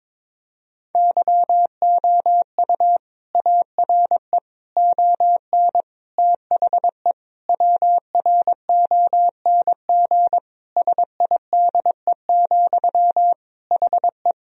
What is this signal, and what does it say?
Morse code, heard as a keyed tone, about 22 wpm, 700 Hz. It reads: YOU ARE ON THE WRONG SIDE, HE
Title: Sentences from Top 1000 Words - Review Part 1 - 22wpm